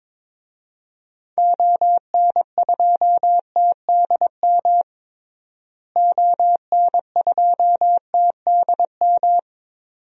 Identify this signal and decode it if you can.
Morse code, heard as a keyed tone, about 22 wpm, 700 Hz. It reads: ON2TDM ON2TDM
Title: DXCC-100 Call Signs - 6 Characters - Repeated 2x - 22wpm